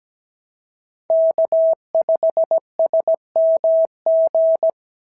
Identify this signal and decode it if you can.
Morse code, heard as a keyed tone, about 16 wpm, 650 Hz. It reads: K5SMG
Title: SOTA QSO Practice 17wpm